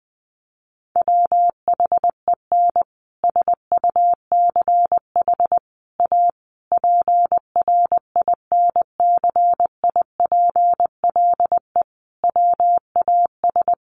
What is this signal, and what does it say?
Morse code, heard as a keyed tone, about 20 wpm, 700 Hz. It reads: WHEN SUCH A PRINCIPLE WAS
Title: Sentences from Top 1400 Words 20wpm